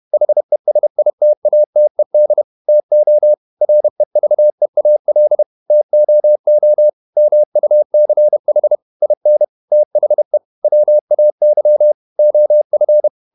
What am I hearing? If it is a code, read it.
Morse code, heard as a keyed tone, about 31 wpm, 600 Hz. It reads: HESITATED TO REVEAL TOO MUCH IN THE WAY OF
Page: Never 31wpm | CW med Gnister